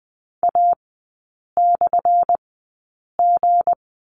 Morse code, heard as a keyed tone, about 20 wpm, spaced as for 8 wpm, 700 Hz. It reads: A/G
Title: 178 Lesson 29e - Slash - Three Characters 20wpm